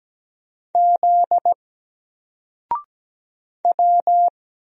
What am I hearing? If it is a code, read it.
Morse code, heard as a keyed tone, about 17 wpm, 700 Hz. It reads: Z W
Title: Single Letter-Number Alt 17wpm